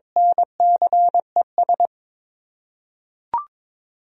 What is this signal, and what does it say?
Morse code, heard as a keyed tone, about 22 wpm, 700 Hz. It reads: NCES
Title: USA Common Names - 22wpm